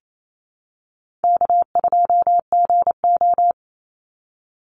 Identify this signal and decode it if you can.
Morse code, heard as a keyed tone, about 28 wpm, 700 Hz. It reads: K2GO
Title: Call Signs Repeated 2x - 28wpm